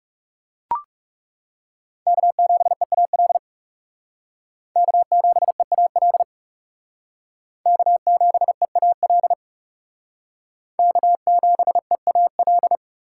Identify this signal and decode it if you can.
Morse code, heard as a keyed tone, about 37 wpm, 700 Hz. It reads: K7EAL K7EAL K7EAL K7EAL
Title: Speed Racing - USA 1x3 Call Signs 30wpm